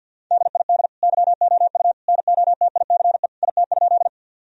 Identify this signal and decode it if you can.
Morse code, heard as a keyed tone, about 52 wpm, 700 Hz. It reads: DID YOU NOTICE IT?
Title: Sentences from Top 1000 Words - Review Part 2 - 50wpm